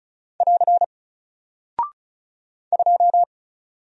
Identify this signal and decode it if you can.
Morse code, heard as a keyed tone, about 36 wpm, 700 Hz. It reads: <AR> 2
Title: Single Character (Letter:Number:Punctuation:Prosign) Alt 35wpm